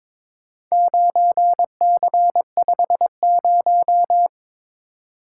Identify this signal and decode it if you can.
Morse code, heard as a keyed tone, about 22 wpm, 700 Hz. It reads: 9C50
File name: practice / 22wpm / Four Letter-Number Combinations - 22wpm